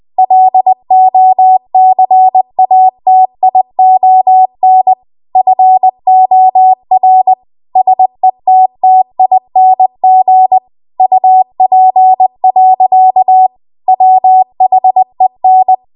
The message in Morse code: LOCATION FOR SETTING UP. WHEN